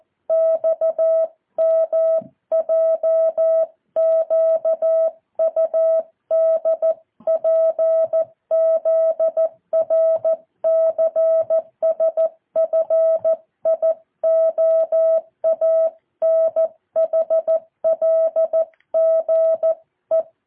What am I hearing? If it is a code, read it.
Morse code, heard as a keyed tone, about 14 wpm, 650 Hz. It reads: XMJQUDPZRCSFIOANHLGE
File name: jh8cbh / 2023cw